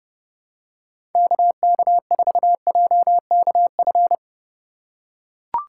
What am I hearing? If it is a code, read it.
Morse code, heard as a keyed tone, about 30 wpm, 700 Hz. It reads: KK4JKF
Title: USA Call Signs - 6 Characters - 2x Character Spacing - 30wpm